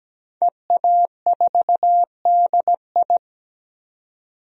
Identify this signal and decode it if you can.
Morse code, heard as a keyed tone, about 17 wpm, 700 Hz. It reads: EA4DI
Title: DXCC-100 Call Signs - 5 Characters - 17wpm